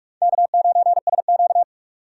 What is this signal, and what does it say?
Morse code, heard as a keyed tone, about 46 wpm, 700 Hz. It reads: K9SQ
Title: USA Call Signs - 4 Characters - 4x Character Spacing - 45wpm